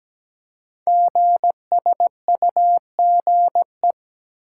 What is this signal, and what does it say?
Morse code, heard as a keyed tone, about 17 wpm, 700 Hz. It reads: GSUGE
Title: Code Groups - 5 Letters - 17wpm